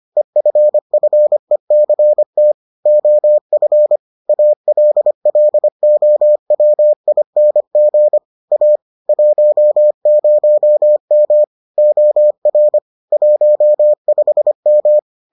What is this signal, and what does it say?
Morse code, heard as a keyed tone, about 25 wpm, 600 Hz. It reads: EFFECT OF ALLOWING A 10M OR 15M